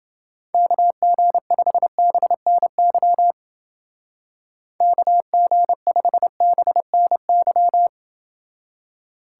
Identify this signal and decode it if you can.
Morse code, heard as a keyed tone, about 28 wpm, 700 Hz. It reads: KG5BNY KG5BNY